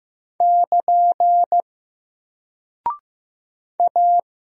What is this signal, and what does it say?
Morse code, heard as a keyed tone, about 15 wpm, 700 Hz. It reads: <KN> A
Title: Single Character (Letter:Number:Punctuation:Prosign) Alt 15wpm